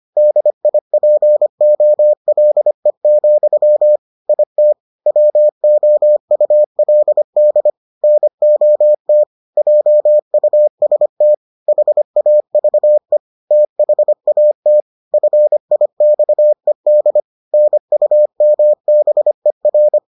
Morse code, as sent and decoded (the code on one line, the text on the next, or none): -.. .. .--. --- .-.. --..-- / .. - / .-- --- ..- .-.. -.. / -. --- - / .--- ..- ... - / .... .- ...- . / - .... .- - / ..-. .. -..- . -.. / -. ..- -- -... .-.
DIPOLE, IT WOULD NOT JUST HAVE THAT FIXED NUMBER